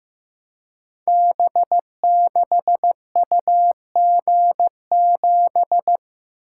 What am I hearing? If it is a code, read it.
Morse code, heard as a keyed tone, about 15 wpm, 700 Hz. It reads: B6UG7
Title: Code Groups - 5 Letter:Numbers - 15wpm